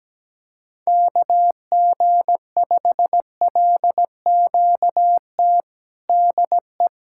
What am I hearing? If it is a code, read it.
Morse code, heard as a keyed tone, about 17 wpm, 700 Hz. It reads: KG5LQT DE
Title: USA Callsign DE Callsign - 6 Characters - 17wpm